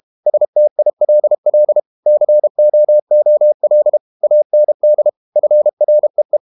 Morse code, heard as a keyed tone, about 32 wpm, 600 Hz. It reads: STILL COOL AND FREE
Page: Never 31wpm | CW med Gnister